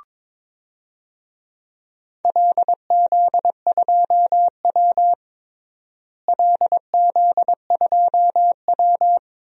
Morse code, heard as a keyed tone, about 22 wpm, 700 Hz. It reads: LZ2W LZ2W
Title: DXCC-100 Call Signs - 4 Characters - Repeated 2x - 22wpm